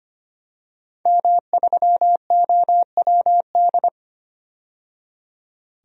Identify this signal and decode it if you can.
Morse code, heard as a keyed tone, about 25 wpm, 700 Hz. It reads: M3OWD
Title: DXCC-100 Call Signs - 5 Characters - 25wpm